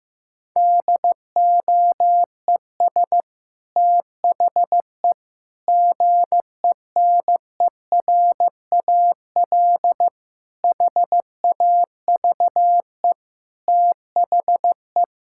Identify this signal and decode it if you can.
Morse code, heard as a keyed tone, about 15 wpm, 700 Hz. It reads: DOES THE GENERAL HAVE THE
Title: Sentences from Top 1000 Words - Review Part 6 - 15wpm